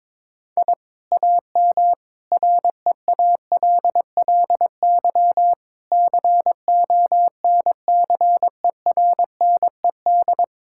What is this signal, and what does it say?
Morse code, heard as a keyed tone, about 22 wpm, 700 Hz. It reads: I AM REALLY CONCERNED
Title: Sentences from Top 1000 Words - Review Part 6 - 22wpm